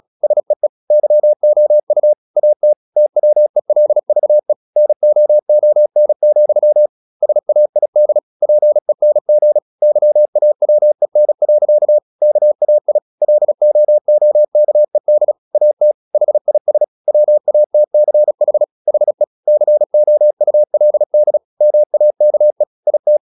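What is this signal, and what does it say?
Morse code, heard as a keyed tone, about 36 wpm, 600 Hz. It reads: SEE YOU AT TWELVE NOON, SAID PENG YAWEN. KAI LOOKED AT HIS WATCH HE COULD MAKE IT
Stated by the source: Morse practice at 35 wpm from CW med Gnister, sending Ken Follett's novel Never in English.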